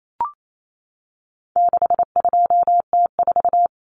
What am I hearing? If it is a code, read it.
Morse code, heard as a keyed tone, about 28 wpm, 700 Hz. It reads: 62T4
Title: Common Cut Numbers - 4-Digit 28wpm